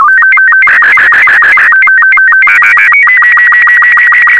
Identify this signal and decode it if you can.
SSTV (slow-scan television), Robot 36.
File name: sstv / R36-VK4MSL